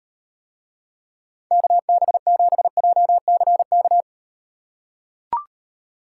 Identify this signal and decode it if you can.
Morse code, heard as a keyed tone, about 38 wpm, 700 Hz. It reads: KB7JCK